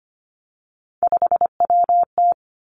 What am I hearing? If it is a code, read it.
Morse code, heard as a keyed tone, about 25 wpm, 700 Hz. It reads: 5WT
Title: Three Challenging Letter-Number Combinations 25wpm